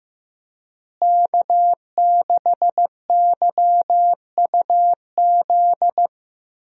Morse code, heard as a keyed tone, about 15 wpm, 700 Hz. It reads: K6YUZ
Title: USA Call Signs - 5 Characters - Repeated 2x - 15wpm